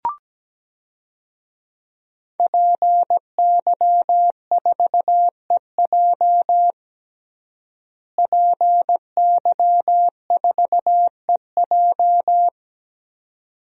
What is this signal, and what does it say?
Morse code, heard as a keyed tone, about 17 wpm, 700 Hz. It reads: PY4EJ PY4EJ
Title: DXCC-100 Call Signs - 5 Characters - Repeated 2x - 17wpm